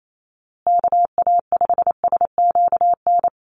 Morse code, heard as a keyed tone, about 28 wpm, 700 Hz. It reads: KA5SQN
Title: USA Call Signs Repeated 3x 28wpm